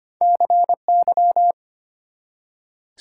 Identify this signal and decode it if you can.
Morse code, heard as a keyed tone, about 34 wpm, spaced as for 30 wpm, 700 Hz. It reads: CY
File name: practice / 25wpm / Speed-Racing - Two Letter-Number Combinations - 25wpm